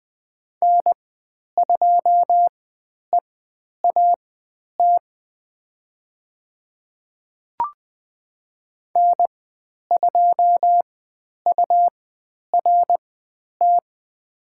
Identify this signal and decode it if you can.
Morse code, heard as a keyed tone, about 20 wpm, spaced as for 10 wpm, 700 Hz. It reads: N2EAT N2URT
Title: 102 Lesson 17g - Letter u - Call Signs 20wpm